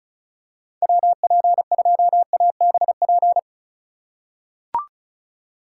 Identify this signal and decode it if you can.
Morse code, heard as a keyed tone, about 34 wpm, 700 Hz. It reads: WP2ABP
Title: USA Call Signs - 6 Characters - 6x Character Spacing - 35wpm